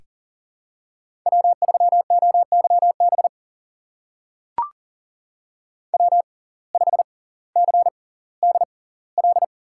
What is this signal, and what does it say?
Morse code, heard as a keyed tone, about 40 wpm, spaced as for 20 wpm, 700 Hz. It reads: W3OYB W5CDL
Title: USA Call Signs - 5 Characters - 6x Character Spacing - 40wpm